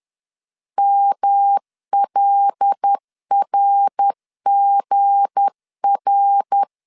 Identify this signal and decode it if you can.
Morse code, heard as a keyed tone, about 10 wpm, 800 Hz. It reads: MLRGR
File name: CW / DET / 1 9 1500C S A 10wpm 4511